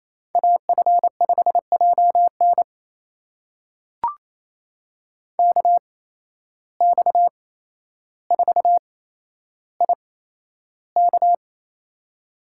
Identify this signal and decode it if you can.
Morse code, heard as a keyed tone, about 28 wpm, spaced as for 12 wpm, 700 Hz. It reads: AF5JN KX4IK
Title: USA Call Signs - 5 Characters - 8x Character Spacing - 28wpm